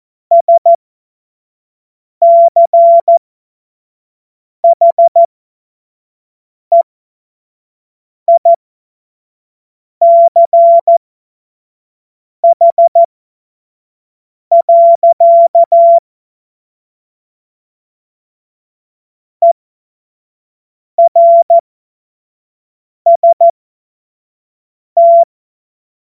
Morse code, heard as a keyed tone, about 14 wpm, spaced as for 5 wpm, 680 Hz. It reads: SCHEICH. ERST
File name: Morsekurs / Witze / Witz 27. April 2021, 25 F70